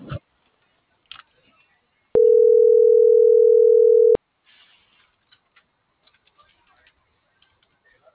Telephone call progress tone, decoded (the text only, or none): ringback tone